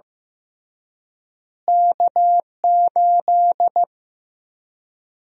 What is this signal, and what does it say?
Morse code, heard as a keyed tone, about 15 wpm, 700 Hz. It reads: K8
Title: USA Call Sign Prefixes - Repeated 2x - 15wpm